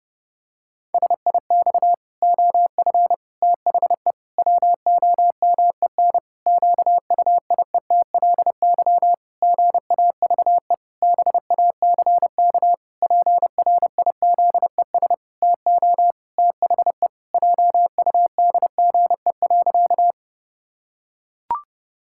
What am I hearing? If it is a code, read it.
Morse code, heard as a keyed tone, about 30 wpm, 700 Hz. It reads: SIX OF THE WOMEN QUIETLY GAVE BACK PRIZES TO THE JUDGE.
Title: Pangram Challenge v2 - 30wpm